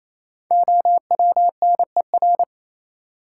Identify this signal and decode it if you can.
Morse code, heard as a keyed tone, about 28 wpm, 700 Hz. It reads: OWNER
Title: Sentences from Top 1600 Words - New Word Drill Alt - 28wpm